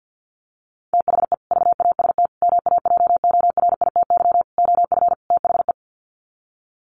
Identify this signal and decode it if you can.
Morse code, heard as a keyed tone, about 50 wpm, 700 Hz. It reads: THE VAST MAJORITY OF THE
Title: Sets of 5 Words Encore 50wpm